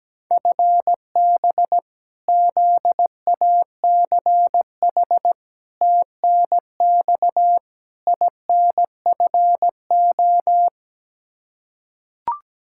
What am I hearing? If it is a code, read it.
Morse code, heard as a keyed tone, about 17 wpm, 700 Hz. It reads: FB ZACH TNX INFO
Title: K1USN SST QSO Practice 17wpm